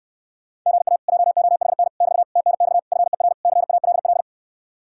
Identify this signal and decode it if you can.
Morse code, heard as a keyed tone, about 66 wpm, 700 Hz. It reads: GA JOHN <BT> TNX FER CALL